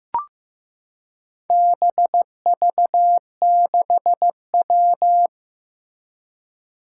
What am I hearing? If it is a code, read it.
Morse code, heard as a keyed tone, about 15 wpm, 700 Hz. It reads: BV6W
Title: Four Challenging Letter-Number Combinations 15wpm